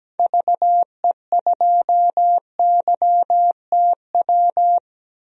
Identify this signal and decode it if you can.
Morse code, heard as a keyed tone, about 17 wpm, 700 Hz. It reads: VE2YTW
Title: DXCC-100 Call Signs - 6 Characters - 17wpm